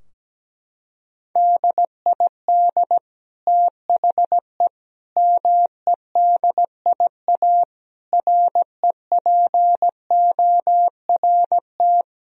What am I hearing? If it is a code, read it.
Morse code, heard as a keyed tone, about 17 wpm, 700 Hz. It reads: DID THE MEDIA REPORT
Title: Sentences from Top 1000 Words - Review Part 4 - 17wpm